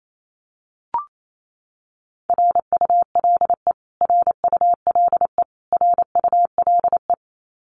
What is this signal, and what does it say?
Morse code, heard as a keyed tone, about 28 wpm, 700 Hz. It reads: RULE RULE RULE
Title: Top 500 Words Repeated 28wpm